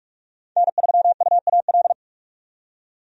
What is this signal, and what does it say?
Morse code, heard as a keyed tone, about 42 wpm, 700 Hz. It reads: N3UAL
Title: USA Call Signs - 5 Characters - 45wpm